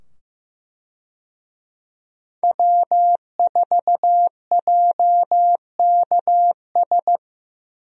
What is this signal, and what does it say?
Morse code, heard as a keyed tone, about 15 wpm, 700 Hz. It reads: W4JKS
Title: USA Call Signs - 5 Characters - 15wpm